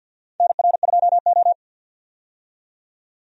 Morse code, heard as a keyed tone, about 47 wpm, 700 Hz. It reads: NR2O
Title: USA Call Signs - 4 Characters - 50wpm